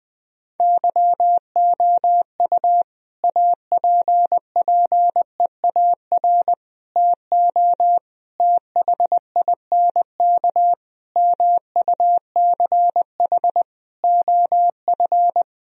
Morse code, heard as a keyed tone, about 20 wpm, 700 Hz. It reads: YOU APPEAR TO THINK MUCH OF
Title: Sentences from Top 1000 Words - Review Part 2 - 20wpm